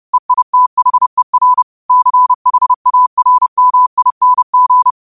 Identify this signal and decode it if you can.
Morse code, heard as a keyed tone, about 30 wpm, 1000 Hz. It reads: EITHER CHARMING
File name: qotdfiles / 30wpm